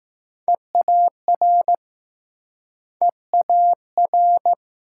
Morse code, heard as a keyed tone, about 20 wpm, 700 Hz. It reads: EAR EAR